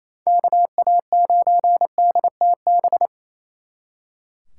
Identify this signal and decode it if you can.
Morse code, heard as a keyed tone, about 37 wpm, 700 Hz. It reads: KA9DTB